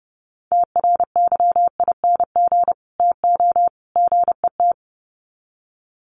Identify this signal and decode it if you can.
Morse code, heard as a keyed tone, about 30 wpm, 700 Hz. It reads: TRYING TO GET
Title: Sets of 3 Words 30wpm